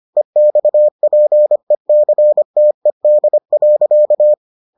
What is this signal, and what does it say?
Morse code, heard as a keyed tone, about 25 wpm, 600 Hz. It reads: EXPECTED.